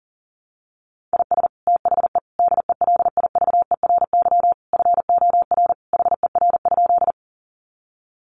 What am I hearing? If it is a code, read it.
Morse code, heard as a keyed tone, about 40 wpm, 700 Hz. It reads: IS THE DELIVERY FOR HER?
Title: Sentences from Top 1500 Words 40wpm